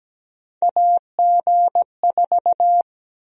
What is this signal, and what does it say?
Morse code, heard as a keyed tone, about 17 wpm, 700 Hz. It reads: AG4
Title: USA Call Sign Prefixes - Repeated 2x - 17wpm